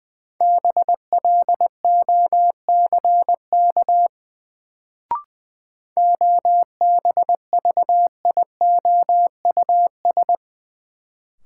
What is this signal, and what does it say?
Morse code, heard as a keyed tone, about 20 wpm, 700 Hz. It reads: BLOCK OBVIOUS
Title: Sentences from Top 1600 Words - New Word Drill Alt - 20wpm